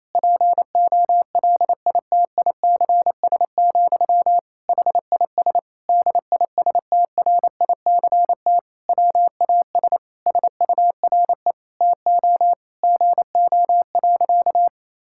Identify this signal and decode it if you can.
Morse code, heard as a keyed tone, about 27 wpm, 700 Hz. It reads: POLITICS, HIS DISTRICT WAS SURE TO GO.
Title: Sentences from Top 1600 Words 28wpm